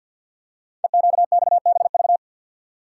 Morse code, heard as a keyed tone, about 52 wpm, 700 Hz. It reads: EQXBV
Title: Code Groups - 5 Letter:Numbers - 50wpm